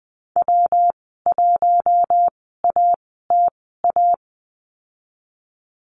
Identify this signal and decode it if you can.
Morse code, heard as a keyed tone, about 20 wpm, spaced as for 14 wpm, 700 Hz. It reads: W1ATA
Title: 122 Lesson 20g - Number 3 - Call Signs 20wpm